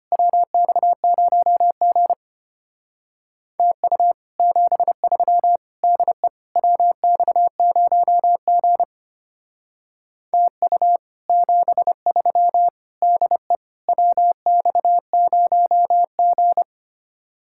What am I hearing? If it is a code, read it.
Morse code, heard as a keyed tone, about 32 wpm, 700 Hz. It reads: WX0G TU 73 DE WX0G TU 73 DE WX0G